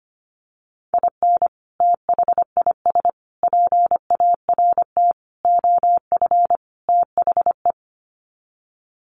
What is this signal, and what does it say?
Morse code, heard as a keyed tone, about 25 wpm, 700 Hz. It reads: IN THIS PART OF THE
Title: Sets of 5 Words 25wpm